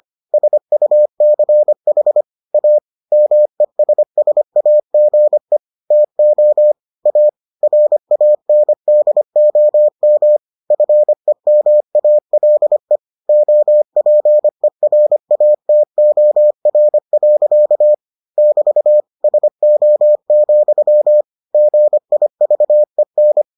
Morse code, heard as a keyed tone, about 25 wpm, 600 Hz. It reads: SUCH A MESSAGE TO A RANDOM FEMALE OPERATOR. <BT> SO, GIVEN